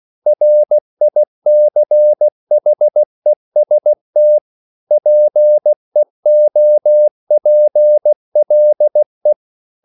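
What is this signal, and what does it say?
Morse code, heard as a keyed tone, about 16 wpm, 600 Hz. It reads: RICHEST PEOPLE
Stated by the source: English quotations sent in CW as practice at 16 words per minute.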